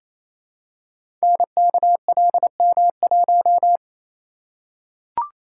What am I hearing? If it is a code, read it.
Morse code, heard as a keyed tone, about 27 wpm, 700 Hz. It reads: NKLM1
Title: Code Groups - 5 Letter:Numbers - 28wpm